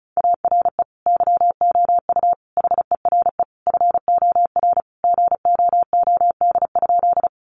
Morse code, heard as a keyed tone, about 34 wpm, 700 Hz. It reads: ARE YOU HERE FOR GOOD?
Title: Sentences from Top 200 Words 35wpm